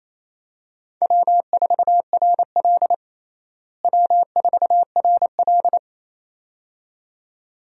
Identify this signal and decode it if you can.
Morse code, heard as a keyed tone, about 28 wpm, 700 Hz. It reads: W4RL W4RL
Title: USA Call Signs - 4 Characters - Repeated 2x - 28wpm